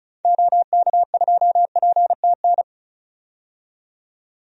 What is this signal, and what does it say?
Morse code, heard as a keyed tone, about 36 wpm, 700 Hz. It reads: OK2PTN
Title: DXCC-100 Call Signs - 6 Characters - 35wpm